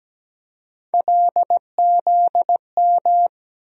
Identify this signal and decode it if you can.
Morse code, heard as a keyed tone, about 17 wpm, 700 Hz. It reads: LZM
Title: USA Call Sign Postfixes - 3 Letters 17wpm